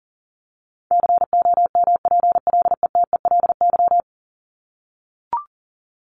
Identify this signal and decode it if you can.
Morse code, heard as a keyed tone, about 40 wpm, 700 Hz. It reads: COMPLETELY
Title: LY Suffix - Top 100 Words with Base Words 40wpm